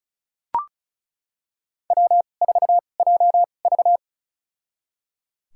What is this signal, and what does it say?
Morse code, heard as a keyed tone, about 36 wpm, spaced as for 25 wpm, 700 Hz. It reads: W4JV
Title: USA Call Signs - 4 Characters - 2x Character Spacing - 35wpm